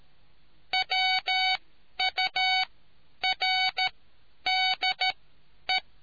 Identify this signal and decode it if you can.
Morse code, heard as a keyed tone, about 13 wpm, spaced as for 9 wpm, 750 Hz. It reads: WURDE